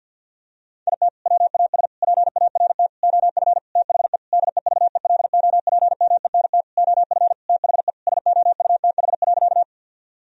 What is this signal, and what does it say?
Morse code, heard as a keyed tone, about 50 wpm, 700 Hz. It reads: IT WAS PART OF THE DEVELOPMENT OF THE SOUTH.
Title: Sentences from Top 1000 Words - Review Part 4 - 50wpm